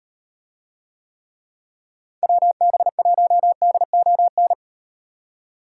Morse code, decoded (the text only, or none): WB1DON